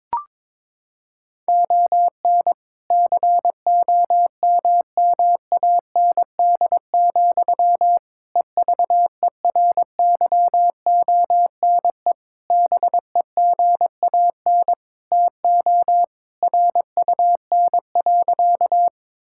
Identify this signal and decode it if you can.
Morse code, heard as a keyed tone, about 22 wpm, 700 Hz. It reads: ON COMMAND, EVERYONE BEGAN TO RUN.
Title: Sentences from Top 1600 Words 22wpm